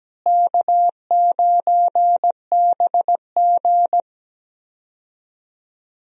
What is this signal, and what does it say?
Morse code, heard as a keyed tone, about 17 wpm, 700 Hz. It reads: K9BG
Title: USA Call Signs - 4 Characters - Repeated 2x - 17wpm